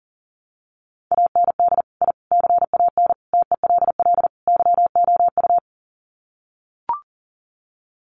Morse code, heard as a keyed tone, about 40 wpm, 700 Hz. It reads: AND I CAN TELL YOU
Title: Sets of 5 Words Encore2 40wpm